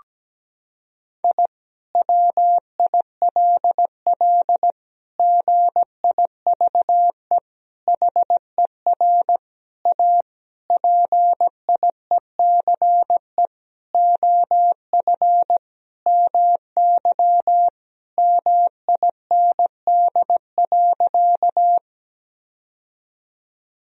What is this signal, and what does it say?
Morse code, heard as a keyed tone, about 17 wpm, 700 Hz. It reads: I WILL GIVE HER A PIECE OF MY MIND.
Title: Sentences from Top 1000 Words - Review Part 2 - 17wpm